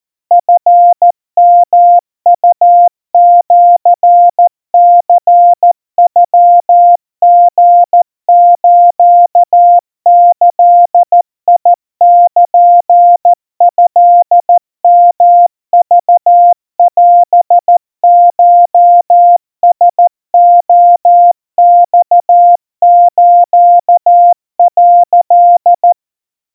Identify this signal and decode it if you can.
Morse code (Japanese Wabun code), heard as a keyed tone, about 14 wpm, 700 Hz. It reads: チヨウシニノリスギルトヨクオコラレマス」
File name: kamyu 50 cpm 700 Hz Crispy